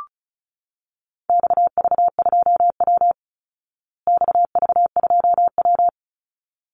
Morse code, heard as a keyed tone, about 34 wpm, 700 Hz. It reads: XV2W XV2W
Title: Call Signs Repeated 3x - 35wpm